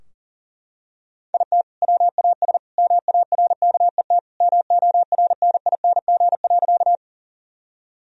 Morse code, heard as a keyed tone, about 37 wpm, 700 Hz. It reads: IT WAS MARKET MORNING.
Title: Sentences from Top 1000 Words - Review Part 3 - 40wpm